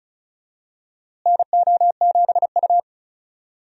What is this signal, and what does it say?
Morse code, heard as a keyed tone, about 36 wpm, 700 Hz. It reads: NO7U
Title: USA Call Signs - 4 Characters - Repeated 2x - 35wpm